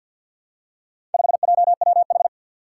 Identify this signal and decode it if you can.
Morse code, heard as a keyed tone, about 50 wpm, 700 Hz. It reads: 5JWH
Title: Four Challenging Letter-Number Combinations 50wpm